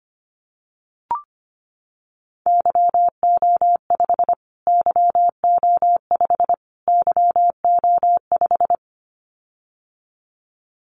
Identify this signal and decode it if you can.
Morse code, heard as a keyed tone, about 25 wpm, 700 Hz. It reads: YO5 YO5 YO5
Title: Callsign Prefixes for Top 25 DXCC - Repeated 3x - 25wpm